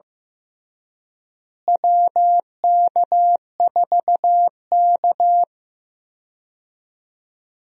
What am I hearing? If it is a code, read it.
Morse code, heard as a keyed tone, about 15 wpm, 700 Hz. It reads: WK4K
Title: USA Call Signs - 4 Characters - Repeated 3x - 15wpm